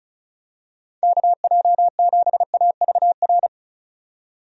Morse code, heard as a keyed tone, about 34 wpm, 700 Hz. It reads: KJ7AVR
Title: USA Call Signs - 6 Characters - 35wpm